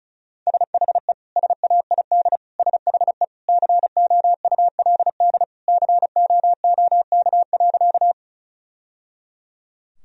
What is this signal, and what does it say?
Morse code, heard as a keyed tone, about 36 wpm, 700 Hz. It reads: SHE SAID SHE COULD COOK.
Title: Sentences from Top 1000 Words - Review Part 4 - 35wpm